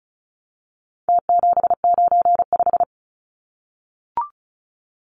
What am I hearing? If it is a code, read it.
Morse code, heard as a keyed tone, about 34 wpm, 700 Hz. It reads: T795
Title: Numbers - Contest Style - 4-Digit 35wpm